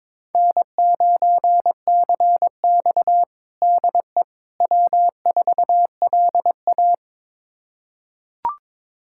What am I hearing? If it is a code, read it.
Morse code, heard as a keyed tone, about 21 wpm, 700 Hz. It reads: N9CX DE W4LA
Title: USA Callsign DE Callsign - 4 Characters - 22wpm